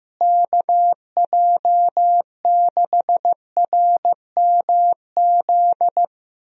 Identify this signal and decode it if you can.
Morse code, heard as a keyed tone, about 15 wpm, 700 Hz. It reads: KJ6RMZ
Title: USA Call Signs - 6 Characters - Repeated 3x - 15wpm